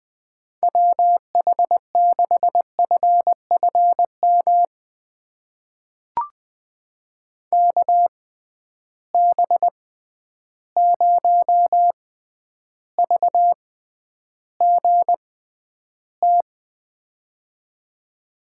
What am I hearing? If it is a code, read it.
Morse code, heard as a keyed tone, about 21 wpm, spaced as for 10 wpm, 700 Hz. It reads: WH6FFM KB0VGT
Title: USA Call Signs - 6 Characters - 6x Character Spacing - 20wpm